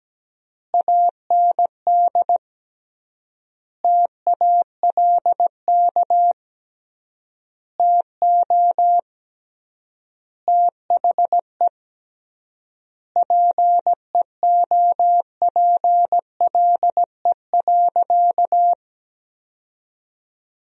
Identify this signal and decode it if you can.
Morse code, heard as a keyed tone, about 17 wpm, 700 Hz. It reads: AND TALK TO THE PEOPLE.
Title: 3x Word-Spacing - Sentences from Top 400 Words - 17wpm